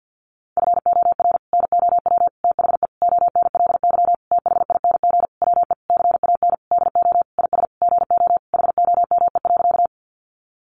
Morse code, heard as a keyed tone, about 50 wpm, 700 Hz. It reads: FOR NOW THE ONLY THING WE CAN DO IS GO HOME.
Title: Sentences from Top 200 Words 50wpm